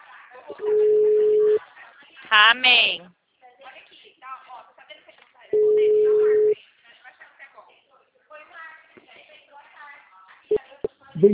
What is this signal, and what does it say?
Ringback tone (Europe, 425 Hz), with speech.
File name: out- -2018- - - .26